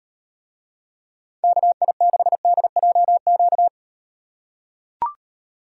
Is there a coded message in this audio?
Yes: Morse code from 1.4 to 3.7 s, heard as a keyed tone, about 38 wpm, 700 Hz. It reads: KI6DJQ